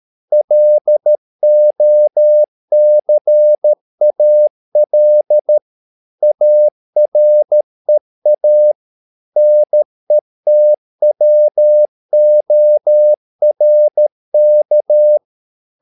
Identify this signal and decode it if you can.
Morse code, heard as a keyed tone, about 13 wpm, 600 Hz. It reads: LOCAL AREA NETWORK
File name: mp3 files / Lexiphiles - 13wpm